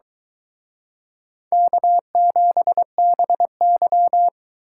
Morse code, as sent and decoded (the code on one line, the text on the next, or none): -.- --... -... -.--
K7BY